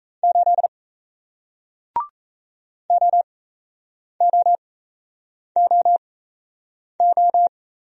Morse code, heard as a keyed tone, about 38 wpm, spaced as for 8 wpm, 700 Hz. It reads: 8 OOOO